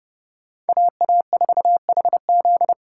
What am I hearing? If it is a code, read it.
Morse code, heard as a keyed tone, about 35 wpm, 700 Hz. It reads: AA4HZ
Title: Speed Racing - USA 2x2 Call Signs 30wpm